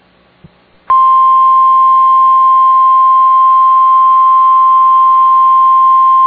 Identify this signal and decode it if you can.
Emergency Alert System (SAME) tones (the 1050 Hz attention tone). It is heard from 1.0 to 5.5 s.